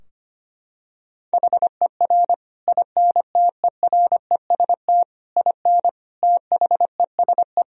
Morse code, heard as a keyed tone, about 25 wpm, 700 Hz. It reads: HER INTEREST IN THESE
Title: Sentences from Top 1000 Words - Review Part 2 - 25wpm